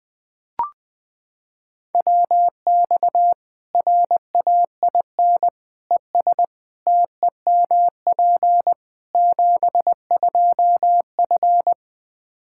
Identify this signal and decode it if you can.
Morse code, heard as a keyed tone, about 20 wpm, 700 Hz. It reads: WX RAIN ES TEMP 72F
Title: K1USN SST QSO Practice 20wpm